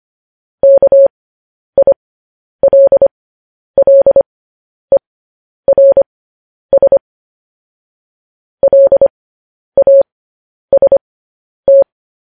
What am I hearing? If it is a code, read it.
Morse code, heard as a keyed tone, about 25 wpm, spaced as for 10 wpm, 550 Hz. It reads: KILLERS LAST